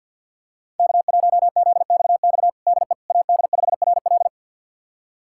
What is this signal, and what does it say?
Morse code, heard as a keyed tone, about 49 wpm, 700 Hz. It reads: K1ZXX DE AD5RL
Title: USA Callsign DE Callsign - 5 Characters - 50wpm